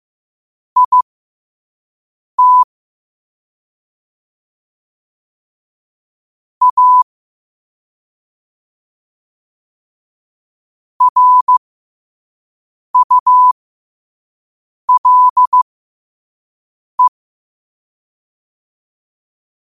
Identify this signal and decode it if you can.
Morse code, heard as a keyed tone, about 15 wpm, spaced as for 5 wpm, 1000 Hz. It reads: IT A RULE